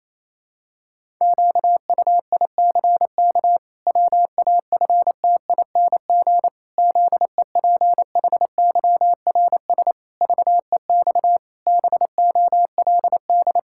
Morse code, heard as a keyed tone, about 28 wpm, 700 Hz. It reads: QUICK WAFTING ZEPHYRS VEX BOLD
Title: Pangram Challenge v2 - 28wpm